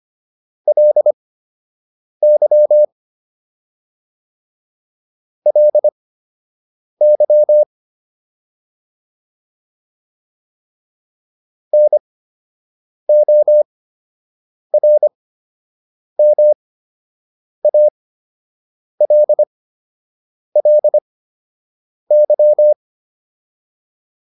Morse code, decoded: LY LY NORMALLY

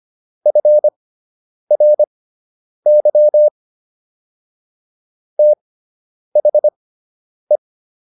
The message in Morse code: FRY THE

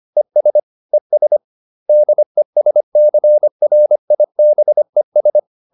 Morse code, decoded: ES ES DESCRIBES

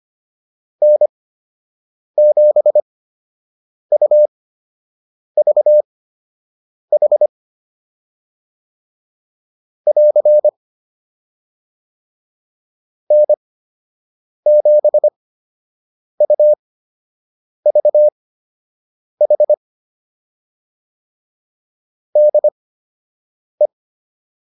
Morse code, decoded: N7UVH <AR> N7UVH DE

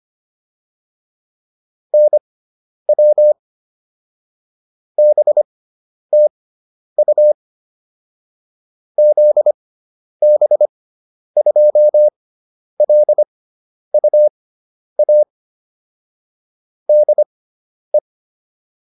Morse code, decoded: NW BTU ZB2LUA DE

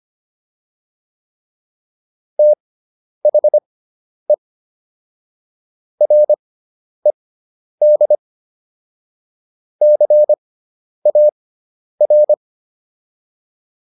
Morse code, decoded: THE RED CAR

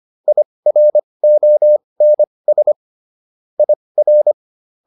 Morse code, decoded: IRONS IR

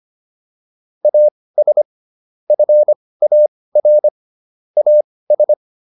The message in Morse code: AS FAR AS

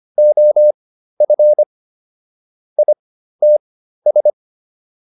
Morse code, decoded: OF ITS

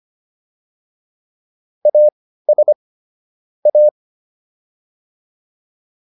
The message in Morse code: AS A

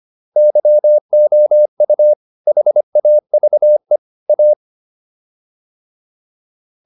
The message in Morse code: YOU HAVE A

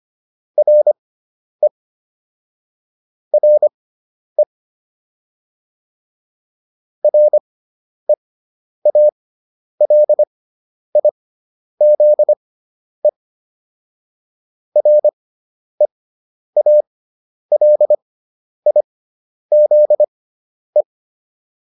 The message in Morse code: RE RE REALIZE REALIZE

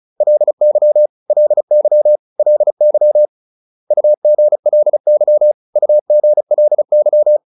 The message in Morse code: LY LY LY UGLY UGLY

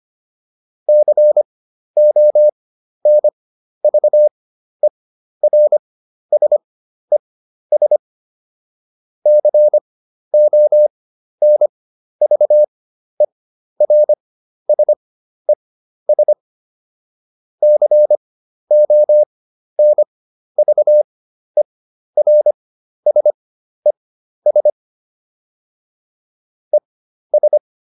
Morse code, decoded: CONVERSES CONVERSES CONVERSES ES

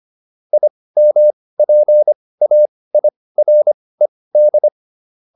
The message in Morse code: IMPAIRED

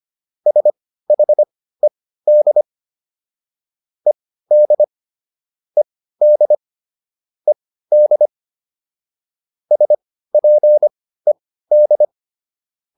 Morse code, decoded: SHED ED ED ED SPED

